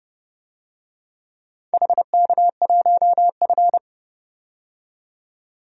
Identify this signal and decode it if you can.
Morse code, heard as a keyed tone, about 30 wpm, 700 Hz. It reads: HK1F
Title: DXCC-100 Call Signs - 4 Characters - 30wpm